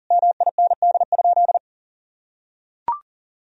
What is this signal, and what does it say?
Morse code, heard as a keyed tone, about 40 wpm, 700 Hz. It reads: MIND?
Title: Sentences from Top 1000 Words - Review Part 2 - 40wpm